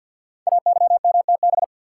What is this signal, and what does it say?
Morse code, heard as a keyed tone, about 48 wpm, 700 Hz. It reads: AYMTB
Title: Code Groups - 5 Letters - 50wpm